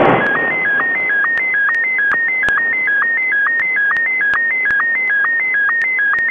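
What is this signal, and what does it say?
SSTV (slow-scan television), Martin 1.
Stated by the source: SSTV Martin 1 transmission from the ISS, RS0ISS, received on 145.800 MHz during a pass, by OK2ZIL & OK2ZBO.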